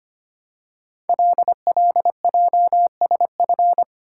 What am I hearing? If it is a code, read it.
Morse code, heard as a keyed tone, about 25 wpm, 700 Hz. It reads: LLJSF
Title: Code Groups - 5 Letters - 25wpm